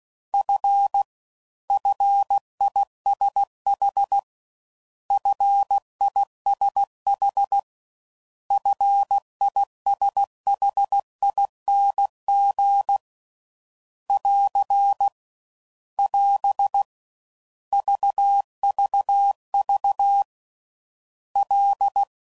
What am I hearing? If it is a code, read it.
Morse code, heard as a keyed tone, about 16 wpm, 800 Hz. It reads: F FISH FISH FISHING <AR> <AS> VVV L